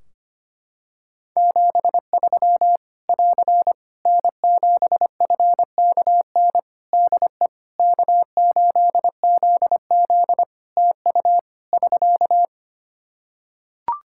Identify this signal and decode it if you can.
Morse code, heard as a keyed tone, about 25 wpm, 700 Hz. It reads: 73 <AR> N7FKN DE K8ZZ TU <SK>
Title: Practice QSO - Exchange 4 - Ending - 25wpm